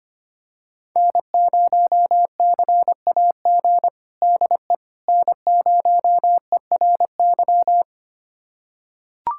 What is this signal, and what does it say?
Morse code, heard as a keyed tone, about 25 wpm, 700 Hz. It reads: N0CAG DE N0ERY
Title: USA Callsign DE Callsign - 5 Characters - 25wpm